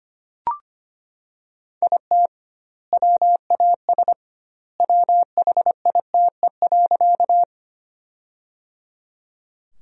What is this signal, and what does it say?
Morse code, heard as a keyed tone, about 25 wpm, 700 Hz. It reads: IT WAS WHITE.
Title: 2x Word-Spacing - Sentences from Top 300 Words - 25wpm